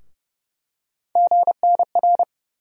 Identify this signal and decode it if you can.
Morse code, heard as a keyed tone, about 32 wpm, 700 Hz. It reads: GNR
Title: USA Call Sign Postfixes - 3 Letters 30wpm